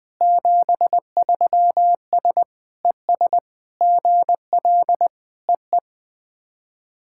Morse code, heard as a keyed tone, about 20 wpm, 700 Hz. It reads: 73S ES GL EE
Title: SOTA QSO Practice 20wpm